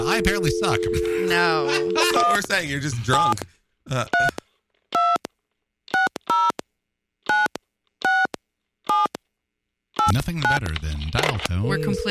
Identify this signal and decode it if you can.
DTMF, heard as telephone touch tones, tones of about 180 ms, with speech. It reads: 17326096789